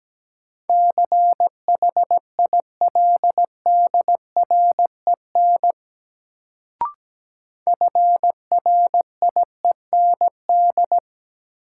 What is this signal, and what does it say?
Morse code, heard as a keyed tone, about 17 wpm, 700 Hz. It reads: CHILDREN FRIEND
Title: Sentences from Top 300 Words - New Word Drill - 17wpm